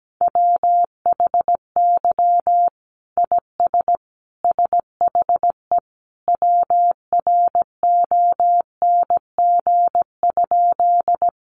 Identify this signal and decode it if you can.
Morse code, heard as a keyed tone, about 17 wpm, 700 Hz. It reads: WHY IS SHE WRONG?
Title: Sentences from Top 900 Words 17wpm